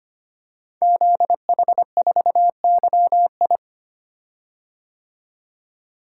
Morse code, heard as a keyed tone, about 25 wpm, 700 Hz. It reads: ZH4YI
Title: Code Groups - 5 Letter:Numbers - 25wpm